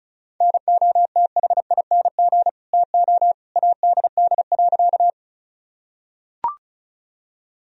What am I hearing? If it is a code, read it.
Morse code, heard as a keyed tone, about 36 wpm, 700 Hz. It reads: NOTHING TO ADD.
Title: Sentences from Top 1000 Words - Review Part 1 - 35wpm